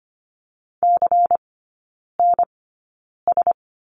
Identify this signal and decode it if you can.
Morse code, heard as a keyed tone, about 25 wpm, spaced as for 9 wpm, 700 Hz. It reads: CNS
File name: course / 25:10wpm / 093 Lesson 16e - Letter c - Three Characters 25wpm